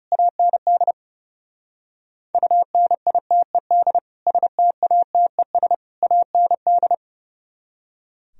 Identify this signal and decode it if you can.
Morse code, heard as a keyed tone, about 37 wpm, 700 Hz. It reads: AND UNITED STATES AND
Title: Speed-Racing - Sets of 3 Words 30wpm